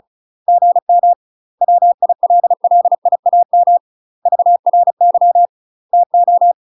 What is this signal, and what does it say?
Morse code, heard as a keyed tone, about 36 wpm, 700 Hz. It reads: GM WILLIAM VRY TO
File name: UK MASTER QSOs TRAINING / 35WPM / UK MASTER QSOs 0032